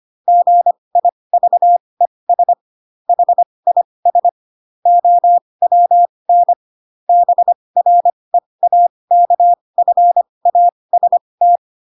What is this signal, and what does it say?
Morse code, heard as a keyed tone, about 25 wpm, spaced as for 20 wpm, 700 Hz. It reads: GIVES HIS OWN BREAKFAST
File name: Pinocchio / Pinocchio ch7 15 25 ik1hge